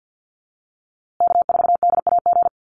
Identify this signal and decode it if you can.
Morse code, heard as a keyed tone, about 50 wpm, 700 Hz. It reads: K4DAG
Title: USA Call Signs Repeated 3x 50wpm